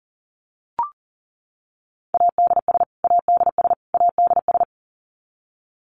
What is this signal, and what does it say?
Morse code, heard as a keyed tone, about 40 wpm, 700 Hz. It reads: ADS ADS ADS
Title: Abbreviations Repeated 3x - 40wpm